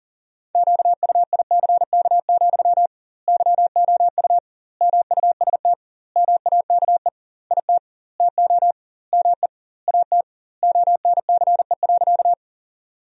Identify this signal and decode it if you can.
Morse code, heard as a keyed tone, about 40 wpm, 700 Hz. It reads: QUICK, YOU MUST MAKE IT TO ME AT ONCE.
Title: 2x Word-Spacing - Sentences from Top 500 Words - 40wpm